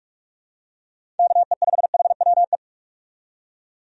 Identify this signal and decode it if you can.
Morse code, heard as a keyed tone, about 46 wpm, 700 Hz. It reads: KE5HWE
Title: USA Call Signs - 6 Characters - 45wpm